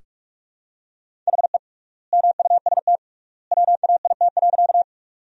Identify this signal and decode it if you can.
Morse code, heard as a keyed tone, about 41 wpm, 700 Hz. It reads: HE MUST WAIT.
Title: 3x Word-Spacing - Sentences from Top 500 Words - 45wpm